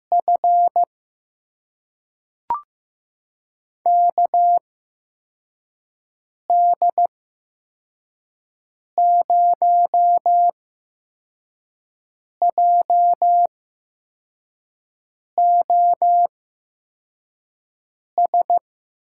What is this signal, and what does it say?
Morse code, heard as a keyed tone, about 15 wpm, spaced as for 4 wpm, 700 Hz. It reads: F KD0JOS
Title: USA Call Signs - 6 Characters - 8x Character Spacing - 15wpm